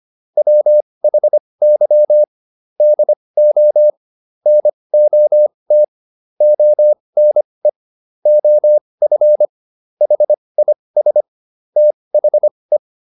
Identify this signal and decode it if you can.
Morse code, heard as a keyed tone, about 25 wpm, spaced as for 20 wpm, 600 Hz. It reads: WHY DO NOT ONE OF HIS THE